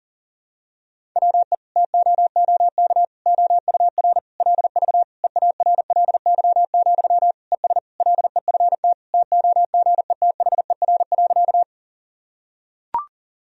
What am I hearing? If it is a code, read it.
Morse code, heard as a keyed tone, about 37 wpm, 700 Hz. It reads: WE TOOK OUR LV EARLY, ES LEFT TOGETHER.
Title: Sentences from Top 400 Words and Abbreviations 40wpm